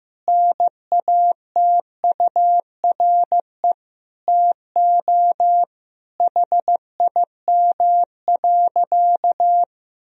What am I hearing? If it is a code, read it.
Morse code, heard as a keyed tone, about 16 wpm, 700 Hz. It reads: NATURE TO HIM.
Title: Sentences from Top 1000 Words - Review Part 1 - 15wpm